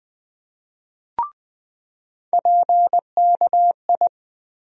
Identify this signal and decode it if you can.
Morse code, heard as a keyed tone, about 20 wpm, 700 Hz. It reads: PKI
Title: USA Call Sign Postfixes - 3 Letters 20wpm